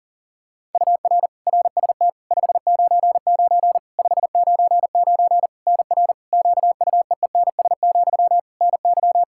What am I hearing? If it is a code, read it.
Morse code, heard as a keyed tone, about 40 wpm, 700 Hz. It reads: UR RST 599 599 NR QUEENS, NY
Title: K1USN SST QSO Practice 40wpm